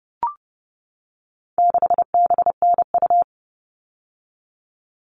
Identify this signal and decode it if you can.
Morse code, heard as a keyed tone, about 30 wpm, 700 Hz. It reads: 6BNU
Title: All Cut Numbers - 4-Digit 30wpm